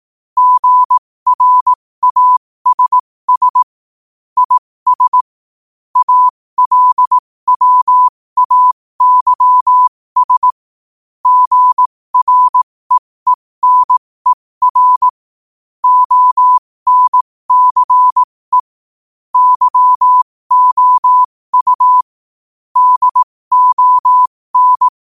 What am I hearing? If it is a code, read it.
Morse code, heard as a keyed tone, about 18 wpm, spaced as for 15 wpm, 1000 Hz. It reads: GRASS IS ALWAYS GREENER ONCE YOU DON
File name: 15wpm